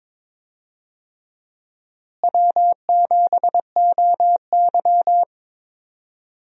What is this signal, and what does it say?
Morse code, heard as a keyed tone, about 22 wpm, 700 Hz. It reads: W7OY
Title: USA Call Signs - 4 Characters - Repeated 3x - 22wpm